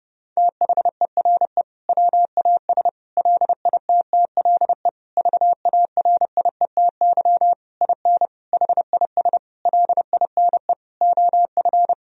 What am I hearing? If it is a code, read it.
Morse code, heard as a keyed tone, about 32 wpm, 700 Hz. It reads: THERE WAS LITTLE VARIETY IN HIS LINE OF
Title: Sentences from Top 1000 Words - Review Part 4 - 30wpm